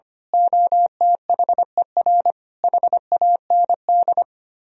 Morse code, heard as a keyed tone, about 24 wpm, 700 Hz. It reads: OTHER HAND
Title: Speed-Racing - Sets of 5 Words 25wpm